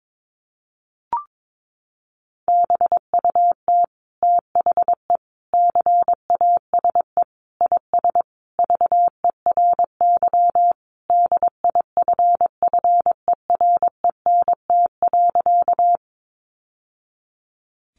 Morse code, heard as a keyed tone, about 22 wpm, 700 Hz. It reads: BUT THE CASE IS VERY DIFFERENT.
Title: Sentences from Top 700 Words 22wpm